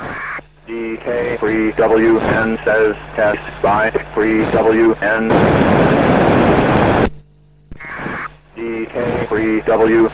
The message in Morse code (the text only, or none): none